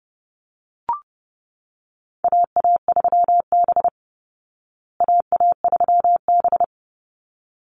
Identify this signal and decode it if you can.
Morse code, heard as a keyed tone, about 30 wpm, 700 Hz. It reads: AA3B AA3B
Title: Call Signs Repeated 3x - 30wpm